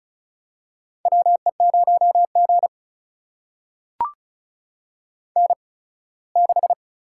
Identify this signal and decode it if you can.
Morse code, heard as a keyed tone, about 36 wpm, spaced as for 21 wpm, 700 Hz. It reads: WE0G N6
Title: USA Call Signs - 8x Character Spacing - 35wpm